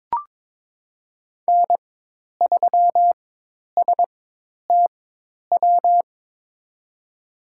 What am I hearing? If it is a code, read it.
Morse code, heard as a keyed tone, about 23 wpm, spaced as for 10 wpm, 700 Hz. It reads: N3STW
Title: USA Call Signs - 5 Characters - 4x Character Spacing - 22wpm